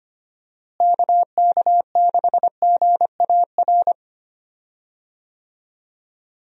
Morse code, heard as a keyed tone, about 25 wpm, 700 Hz. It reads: KK6GAR
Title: USA Call Signs - 6 Characters - Repeated 2x - 25wpm